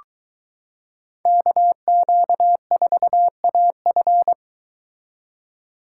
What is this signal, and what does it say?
Morse code, heard as a keyed tone, about 23 wpm, 700 Hz. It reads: KQ4AF